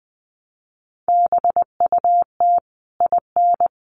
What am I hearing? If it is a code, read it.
Morse code, heard as a keyed tone, about 20 wpm, 700 Hz. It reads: BUT IN
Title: Sentences from Top 1500 Words 20wpm